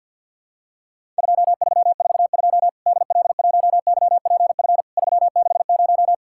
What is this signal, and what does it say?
Morse code, heard as a keyed tone, about 50 wpm, 700 Hz. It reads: 2342 DL1YPF 360
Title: Worked All Europe DX Contest - QTC Exchange - 50wpm